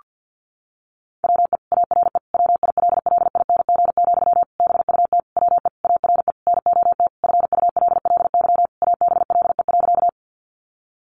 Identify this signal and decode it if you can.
Morse code, heard as a keyed tone, about 50 wpm, 700 Hz. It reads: WE ARE WILLING, BUT WE ARE NOT FULLY ABLE.
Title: Sentences from Top 1300 Words 50wpm